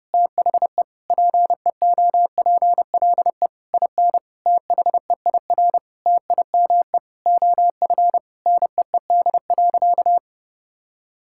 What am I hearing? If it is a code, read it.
Morse code, heard as a keyed tone, about 32 wpm, 700 Hz. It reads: THE PEOPLE IN THEIR TIME OF NEED.
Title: Sentences from Top 1000 Words - Review Part 5 - 30wpm